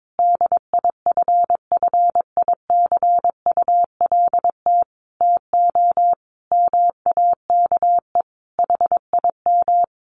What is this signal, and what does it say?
Morse code, heard as a keyed tone, about 22 wpm, 700 Hz. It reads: DIFFICULT TO MAKE HIM
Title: Sentences from Top 600 Words 22wpm